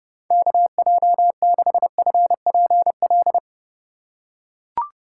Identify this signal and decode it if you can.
Morse code, heard as a keyed tone, about 30 wpm, 700 Hz. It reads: KJ6FPL
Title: USA Call Signs - 6 Characters - 2x Character Spacing - 30wpm